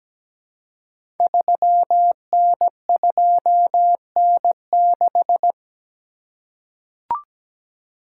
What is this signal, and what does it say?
Morse code, heard as a keyed tone, about 17 wpm, 690 Hz. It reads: 3N2N6
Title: Code Groups - 5 Letter:Numbers - 17wpm